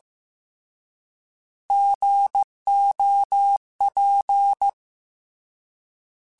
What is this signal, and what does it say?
Morse code, heard as a keyed tone, about 15 wpm, 800 Hz. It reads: GOP